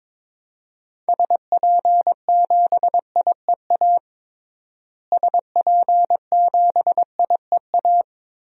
Morse code, heard as a keyed tone, about 22 wpm, 700 Hz. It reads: SP7IEA SP7IEA
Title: DXCC-100 Call Signs - 6 Characters - Repeated 3x - 22wpm